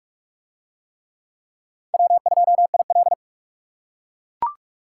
Morse code, heard as a keyed tone, about 36 wpm, 700 Hz. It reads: W2IL
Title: Speed Racing - USA 1x2 Call Signs 30wpm